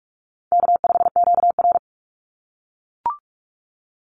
Morse code, heard as a keyed tone, about 44 wpm, 700 Hz. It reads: K5QR
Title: NA SOTA Chaser Callsigns 45wpm